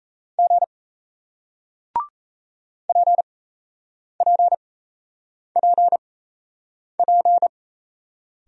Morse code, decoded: G PPPP

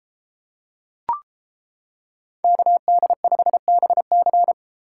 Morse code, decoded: KD5BC

